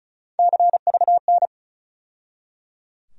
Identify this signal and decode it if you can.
Morse code, heard as a keyed tone, about 34 wpm, 700 Hz. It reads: CVN
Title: USA Call Sign Postfixes - 3 Letters 35wpm